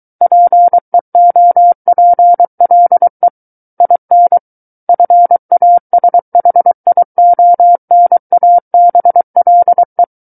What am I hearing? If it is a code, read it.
Morse code, heard as a keyed tone, about 23 wpm, 700 Hz. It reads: PEOPLE IN FASHIONABLE
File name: Book One / 14-Chapter 14 - 23 WPM